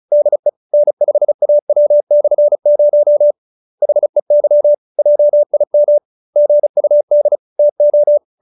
Morse code, heard as a keyed tone, about 34 wpm, 590 Hz. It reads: DE N5AW/0 HEY JIM GUD TO
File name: QSO312 35